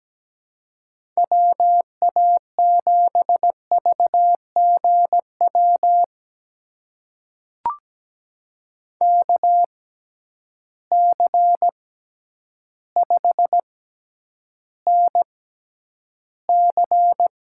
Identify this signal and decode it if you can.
Morse code, heard as a keyed tone, about 17 wpm, spaced as for 9 wpm, 700 Hz. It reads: WA7VGW KC5NC
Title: USA Call Signs - 6 Characters - 6x Character Spacing - 17wpm